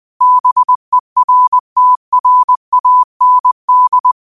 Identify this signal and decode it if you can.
Morse code, heard as a keyed tone, about 20 wpm, 1000 Hz. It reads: BERTRAND